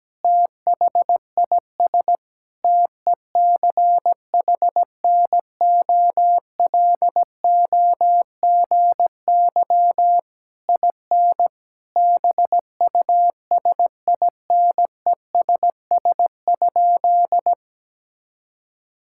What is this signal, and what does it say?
Morse code, heard as a keyed tone, about 17 wpm, 700 Hz. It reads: THIS TECHNOLOGY IN BUSINESS?
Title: Sentences from Top 1000 Words - Review Part 5 - 17wpm